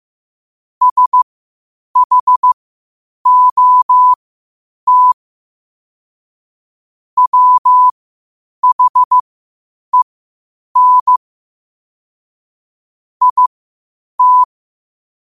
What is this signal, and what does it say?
Morse code, heard as a keyed tone, about 15 wpm, spaced as for 8 wpm, 1000 Hz. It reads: SHOT WHEN IT
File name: qotdfiles / 8wpm